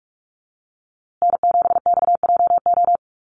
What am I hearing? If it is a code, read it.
Morse code, heard as a keyed tone, about 44 wpm, 700 Hz. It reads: N7XJO
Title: USA Call Signs Repeated 2x 45wpm